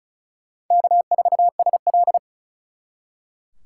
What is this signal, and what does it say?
Morse code, heard as a keyed tone, about 44 wpm, 700 Hz. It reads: K4SL